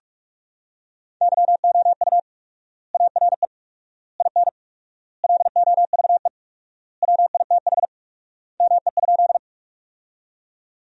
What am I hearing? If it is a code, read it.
Morse code, heard as a keyed tone, about 46 wpm, 700 Hz. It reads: YOU ARE IN LOVE WITH ME?
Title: 4x Word-Spacing - Sentences from Top 400 Words - 45wpm